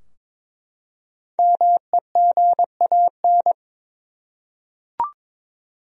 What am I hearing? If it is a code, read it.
Morse code, heard as a keyed tone, about 22 wpm, 700 Hz. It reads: MEGAN
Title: USA Common Names - Max 5 Letters - 22wpm